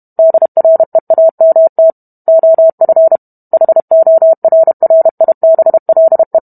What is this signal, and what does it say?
Morse code, heard as a keyed tone, about 32 wpm, 650 Hz. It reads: DREAMT OF HORRIBLE